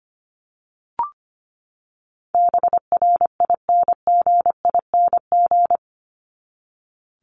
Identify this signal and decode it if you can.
Morse code, heard as a keyed tone, about 25 wpm, 700 Hz. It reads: BRINGING
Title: ING Suffix - Top 100 Words with Base Words 25wpm